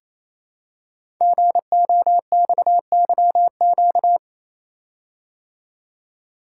Morse code, heard as a keyed tone, about 28 wpm, 700 Hz. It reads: GOXYQ
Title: Code Groups - 5 Letter:Numbers - 28wpm